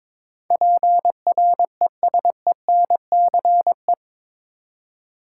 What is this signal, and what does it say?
Morse code, heard as a keyed tone, about 22 wpm, 700 Hz. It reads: PRESENCE
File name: practice / 22wpm / Eight Letter Words - 22wpm